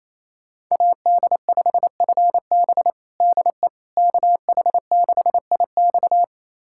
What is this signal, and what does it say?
Morse code, heard as a keyed tone, about 27 wpm, 700 Hz. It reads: AD5FB DE KH6IX
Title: USA Callsign DE Callsign - 5 Characters - 28wpm